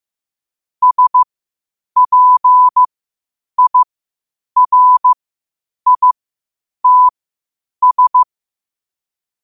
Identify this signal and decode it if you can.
Morse code, heard as a keyed tone, about 15 wpm, spaced as for 8 wpm, 1000 Hz. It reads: SPIRITS
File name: qotdfiles / 8wpm